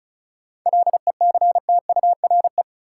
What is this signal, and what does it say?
Morse code, heard as a keyed tone, about 36 wpm, 700 Hz. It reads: LECTURE
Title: Sentences from Top 1600 Words - New Word Drill Alt - 35wpm